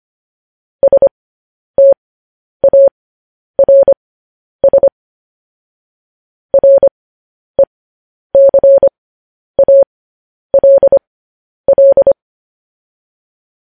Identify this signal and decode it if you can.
Morse code, heard as a keyed tone, about 25 wpm, spaced as for 10 wpm, 550 Hz. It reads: STARS RECALL
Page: news headlines in morse code at 10 wpm